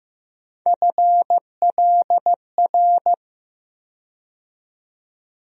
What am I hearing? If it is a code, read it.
Morse code, heard as a keyed tone, about 15 wpm, 700 Hz. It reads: FLR
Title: USA Call Sign Postfixes - 3 Letters 15wpm